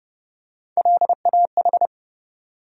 Morse code, heard as a keyed tone, about 32 wpm, 700 Hz. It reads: LAH
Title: USA Call Sign Postfixes - 3 Letters 30wpm